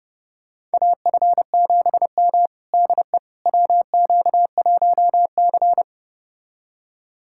Morse code, heard as a keyed tone, about 30 wpm, 700 Hz. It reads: AF7M DE WQ1C
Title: USA Callsign DE Callsign - 4 Characters - 30wpm